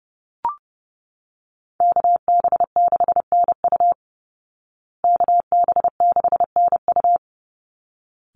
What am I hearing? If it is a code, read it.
Morse code, heard as a keyed tone, about 30 wpm, 700 Hz. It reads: KB6NU KB6NU
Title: Call Signs Repeated 3x - 30wpm